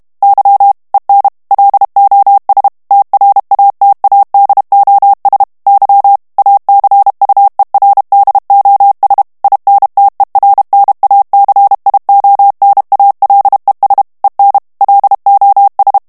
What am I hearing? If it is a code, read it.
Morse code, heard as a keyed tone, about 32 wpm, 800 Hz. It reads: Y EN LOS TRATADOS Y ACUERDOS INTERNACIONALES EN LOS
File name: morse mp3 / ESPAGNOL TEXTE EN CLAIR 2 VITESSE 30 MOTS MINUTES